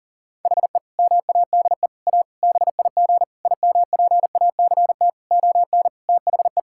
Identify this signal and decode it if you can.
Morse code, heard as a keyed tone, about 40 wpm, 700 Hz. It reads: HE MADE A BIG IMPACT ON THE
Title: Sentences from Top 1000 Words - Review Part 6 - 40wpm